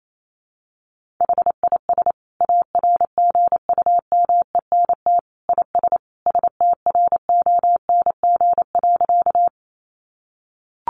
Morse code, heard as a keyed tone, about 28 wpm, 700 Hz. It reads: HIS ARGUMENT IS STRONG.
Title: Sentences from Top 1400 Words 28wpm